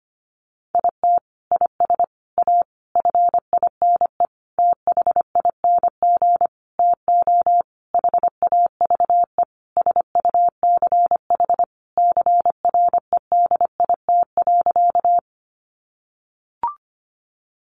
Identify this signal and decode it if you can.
Morse code, heard as a keyed tone, about 25 wpm, 700 Hz. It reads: IT IS A FINE THING TO HAVE SUCH CREDIT.
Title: Sentences from Top 1000 Words 25wpm